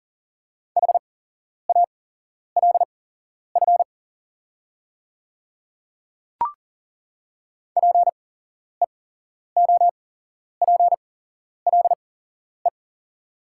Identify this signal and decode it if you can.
Morse code, heard as a keyed tone, about 40 wpm, spaced as for 11 wpm, 690 Hz. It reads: HALF PEOPLE
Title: 8x Character Spacing - Top 500 Words - 40wpm